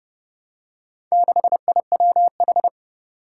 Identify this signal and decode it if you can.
Morse code, heard as a keyed tone, about 32 wpm, 700 Hz. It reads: 6IWH
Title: Four Challenging Letter-Number Combinations 30wpm